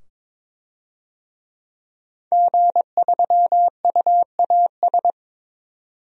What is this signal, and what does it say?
Morse code, heard as a keyed tone, about 22 wpm, 700 Hz. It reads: G3UAS
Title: DXCC-100 Call Signs - 5 Characters - 22wpm